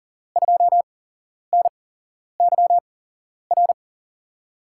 Morse code, heard as a keyed tone, about 42 wpm, spaced as for 11 wpm, 700 Hz. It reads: 2NYR
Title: USA Call Signs - 6 Characters - 8x Character Spacing - 40wpm